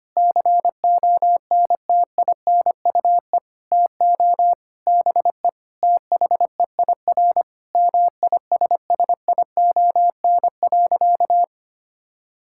Morse code, heard as a keyed tone, about 25 wpm, 700 Hz. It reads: CONTINUE TO BE THEIR MISSION.
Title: Sentences from Top 1600 Words 25wpm